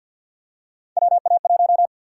Morse code, heard as a keyed tone, about 53 wpm, 700 Hz. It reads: WA1
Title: USA Call Sign Prefixes - Repeated 2x - 50wpm